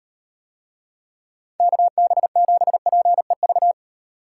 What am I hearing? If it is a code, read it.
Morse code, heard as a keyed tone, about 36 wpm, 700 Hz. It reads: KB7PEV